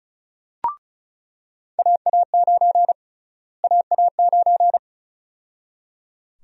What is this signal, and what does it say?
Morse code, heard as a keyed tone, about 34 wpm, 700 Hz. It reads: AA9 AA9
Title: USA Call Sign Prefixes - Repeated 2x - 35wpm